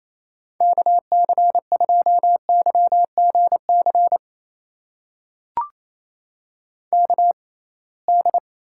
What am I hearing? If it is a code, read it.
Morse code, heard as a keyed tone, about 28 wpm, spaced as for 21 wpm, 700 Hz. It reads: KC2YGC KD
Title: USA Call Signs - 6 Characters - 6x Character Spacing - 28wpm